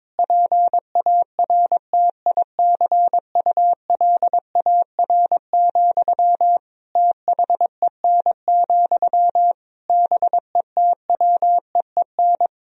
Morse code, heard as a keyed tone, about 22 wpm, 700 Hz. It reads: PARTICULAR, THEN, BETWEEN
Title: Sentences from Top 1000 Words - Review Part 4 - 22wpm